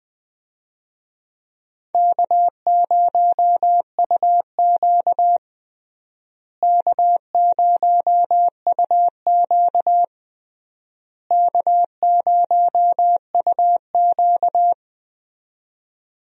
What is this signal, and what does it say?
Morse code, heard as a keyed tone, about 21 wpm, 700 Hz. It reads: K0UQ K0UQ K0UQ
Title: USA Call Signs - 4 Characters - Repeated 3x - 20wpm